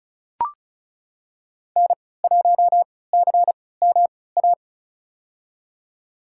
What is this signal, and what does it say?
Morse code, heard as a keyed tone, about 34 wpm, spaced as for 20 wpm, 700 Hz. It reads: N1CMA
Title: USA Call Signs - 5 Characters - 3x Character Spacing - 35wpm